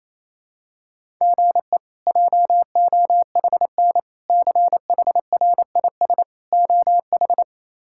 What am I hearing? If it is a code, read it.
Morse code, heard as a keyed tone, about 28 wpm, 700 Hz. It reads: GE JOHN CHRIS OH
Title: K1USN SST QSO Practice - 28wpm